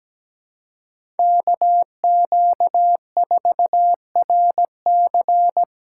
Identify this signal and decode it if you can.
Morse code, heard as a keyed tone, about 17 wpm, 700 Hz. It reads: KQ4RC
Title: USA Call Signs - 5 Characters - 2x Character Spacing - 17wpm